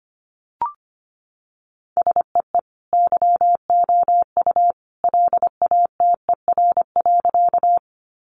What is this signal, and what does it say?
Morse code, heard as a keyed tone, about 25 wpm, 700 Hz. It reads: SEE YOU LATER.
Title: Sentences from Top 700 Words 25wpm